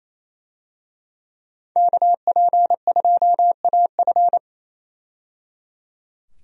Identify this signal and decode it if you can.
Morse code, heard as a keyed tone, about 27 wpm, 700 Hz. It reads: KP2AF
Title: USA Call Signs - 5 Characters - 28wpm